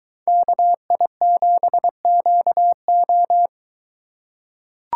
Morse code, heard as a keyed tone, about 22 wpm, 700 Hz. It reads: KI7QO